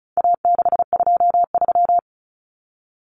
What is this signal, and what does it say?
Morse code, heard as a keyed tone, about 34 wpm, 700 Hz. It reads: A623
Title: Common Cut Numbers - 4-Digit 35wpm